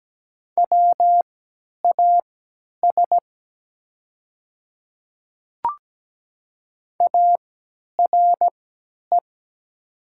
Morse code, heard as a keyed tone, about 17 wpm, spaced as for 10 wpm, 700 Hz. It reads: WAS ARE
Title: 3x Character Spacing - Top 200 Words - 17wpm